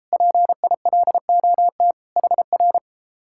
Morse code, heard as a keyed tone, about 33 wpm, 700 Hz. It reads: PILOT HR